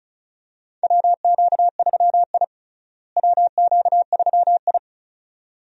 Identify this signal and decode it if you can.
Morse code, heard as a keyed tone, about 34 wpm, 700 Hz. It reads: WQ3I WQ3I
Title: USA Call Signs - 4 Characters - Repeated 2x - 35wpm